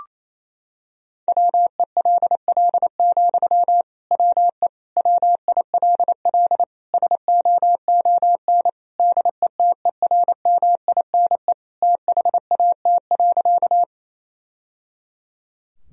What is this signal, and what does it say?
Morse code, heard as a keyed tone, about 28 wpm, 700 Hz. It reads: WELL, WE WILL SOON DETERMINE THAT.
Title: Sentences from Top 1000 Words - Review Part 4 - 28wpm